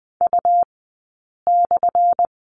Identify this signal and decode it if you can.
Morse code, heard as a keyed tone, about 20 wpm, spaced as for 8 wpm, 700 Hz. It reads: U/
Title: 177 Lesson 29d - Slash - Two Characters 20wpm